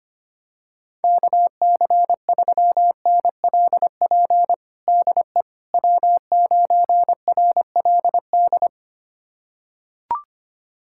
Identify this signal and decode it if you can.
Morse code, heard as a keyed tone, about 25 wpm, 700 Hz. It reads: KC3NLP DE W9RLD
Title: USA Callsign DE Callsign 25wpm